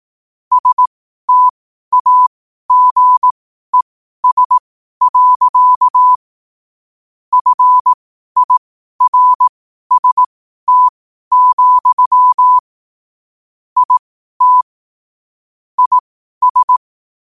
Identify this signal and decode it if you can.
Morse code, heard as a keyed tone, about 18 wpm, spaced as for 12 wpm, 1000 Hz. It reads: STAGES. FIRST, IT IS
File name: qotdfiles / 12wpm